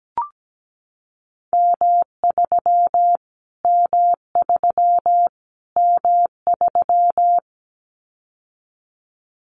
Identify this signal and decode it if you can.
Morse code, heard as a keyed tone, about 17 wpm, 700 Hz. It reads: M3 M3 M3
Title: Callsign Prefixes for Top 25 DXCC - Repeated 3x - 17wpm